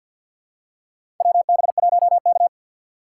Morse code, heard as a keyed tone, about 53 wpm, 690 Hz. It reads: WB1K
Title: USA Call Signs - 6x Character Spacing - 50wpm